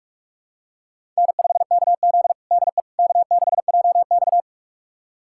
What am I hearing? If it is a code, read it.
Morse code, heard as a keyed tone, about 46 wpm, 700 Hz. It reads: N5KZ DE K6JX
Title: USA Callsign DE Callsign - 4 Characters - 45wpm